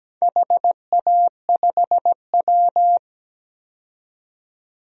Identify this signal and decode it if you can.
Morse code, heard as a keyed tone, about 17 wpm, 700 Hz. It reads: HA5W
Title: DXCC-100 Call Signs - 4 Characters - 17wpm